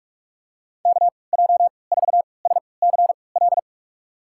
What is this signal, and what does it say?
Morse code, heard as a keyed tone, about 46 wpm, spaced as for 26 wpm, 700 Hz. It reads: KJ4SCL
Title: USA Call Signs - 6 Characters - 3x Character Spacing - 45wpm